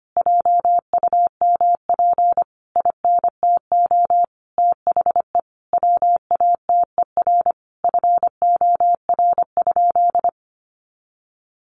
Morse code, heard as a keyed tone, about 25 wpm, 700 Hz. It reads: JUMP INTO THE WATER FOR?
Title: Sentences from Top 1300 Words 25wpm